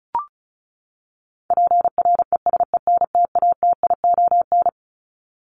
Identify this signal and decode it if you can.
Morse code, heard as a keyed tone, about 34 wpm, 700 Hz. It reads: PRESENTATION
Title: Top 1501-1750 Words 35wpm